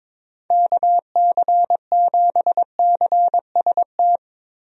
Morse code, heard as a keyed tone, about 22 wpm, 700 Hz. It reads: KC7CST
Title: USA Call Signs - 6 Characters - 22wpm